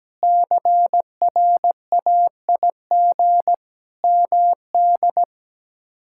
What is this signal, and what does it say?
Morse code, heard as a keyed tone, about 17 wpm, 700 Hz. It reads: CRAIG MD
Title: K1USN SST QSO Practice - 17wpm